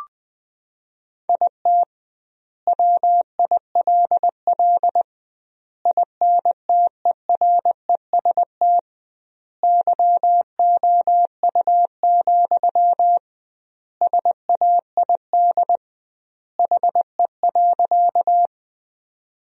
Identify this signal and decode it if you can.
Morse code, heard as a keyed tone, about 20 wpm, 700 Hz. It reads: IT WILL INTEREST YOU, SAID HE.
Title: 2x Word-Spacing - Sentences from Top 400 Words - 20wpm